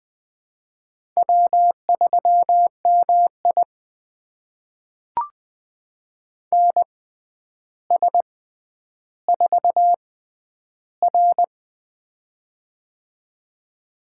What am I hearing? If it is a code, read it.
Morse code, heard as a keyed tone, about 20 wpm, spaced as for 10 wpm, 700 Hz. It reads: W3MI NS4R
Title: USA Call Signs - 4 Characters - 6x Character Spacing - 20wpm